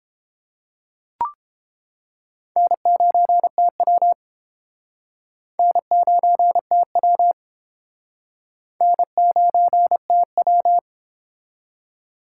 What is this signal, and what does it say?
Morse code, heard as a keyed tone, about 29 wpm, 700 Hz. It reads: N9TW N9TW N9TW